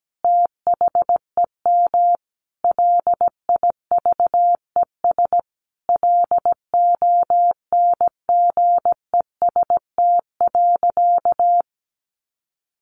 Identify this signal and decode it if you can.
Morse code, heard as a keyed tone, about 17 wpm, 700 Hz. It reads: THEM LIVES LONGEST.
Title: A Sound Of Thunder 17wpm